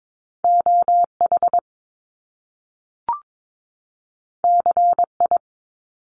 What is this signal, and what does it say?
Morse code, heard as a keyed tone, about 22 wpm, 700 Hz. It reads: OH CI
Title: Top Two Letter Combinations in Words 22wpm